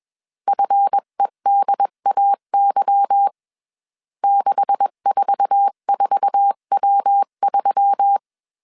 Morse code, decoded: FEDAY 644W3

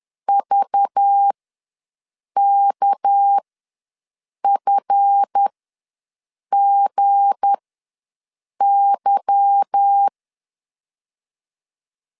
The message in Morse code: VKFGY

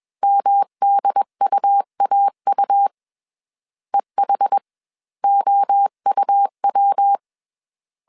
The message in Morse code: MDUAU EHOUW

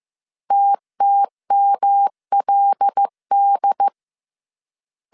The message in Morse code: TTMLD

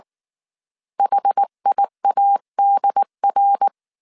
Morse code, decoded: HIADR

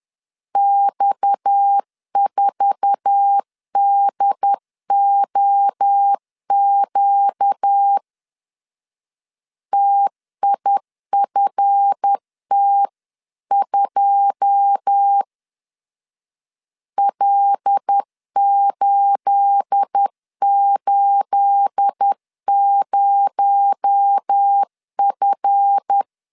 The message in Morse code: X4DOQ TIFT2 L880F

